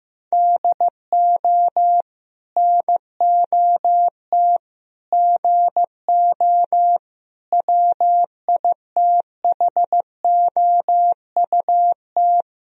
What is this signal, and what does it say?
Morse code, heard as a keyed tone, about 15 wpm, 700 Hz. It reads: DO NOT GO WITHOUT
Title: Sentences from Top 1000 Words - Review Part 3 - 15wpm